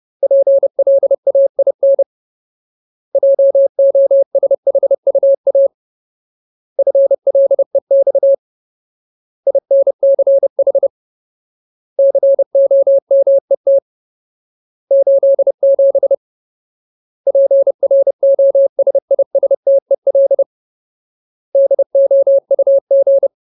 Morse code, heard as a keyed tone, about 30 wpm, 550 Hz. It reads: PLAIN JOSHUA FLEX INCH COMET 87 PROSISTEL DOUG
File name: Ultimate Word List / 30WPM Ultimate